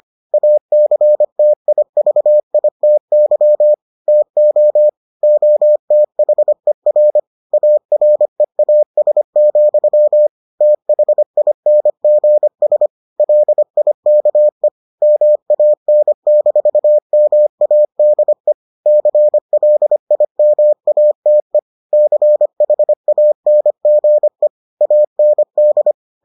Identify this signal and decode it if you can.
Morse code, heard as a keyed tone, about 25 wpm, 600 Hz. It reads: ACTIVITY TO OTHER AREAS, THINGS LIKE MAN-MADE CLIMATE CHANGE AND